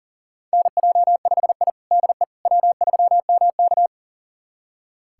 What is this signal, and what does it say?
Morse code, heard as a keyed tone, about 40 wpm, 700 Hz. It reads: NJ5I DE W3MK
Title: USA Callsign DE Callsign - 4 Characters - 40wpm